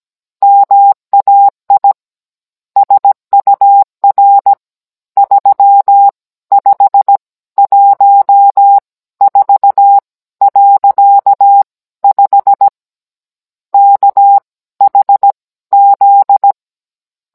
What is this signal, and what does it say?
Morse code, heard as a keyed tone, about 17 wpm, spaced as for 14 wpm, 800 Hz. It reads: MAI SUR 3514.5 KHZ
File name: Cours-NR19-16mm